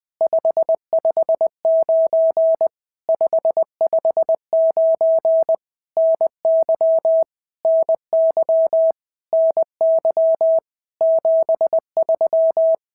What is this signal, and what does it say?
Morse code, heard as a keyed tone, about 20 wpm, 650 Hz. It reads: 559 559 NY NY NY 73
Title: POTA QSO Practice - 20wpm